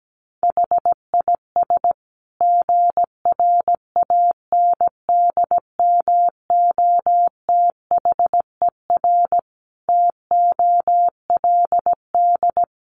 Morse code, heard as a keyed tone, about 17 wpm, 700 Hz. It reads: HIS GRANDMOTHER TOLD
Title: Sentences from Top 1200 Words 17wpm